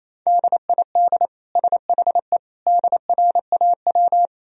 Morse code, heard as a keyed tone, about 28 wpm, 700 Hz. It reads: DID SHE DRAW
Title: Sentences from Top 1000 Words - Review Part 2 - 28wpm